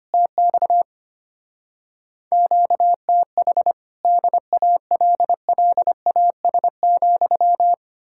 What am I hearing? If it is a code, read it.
Morse code, heard as a keyed tone, about 33 wpm, 700 Hz. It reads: TX QTH DALLAS,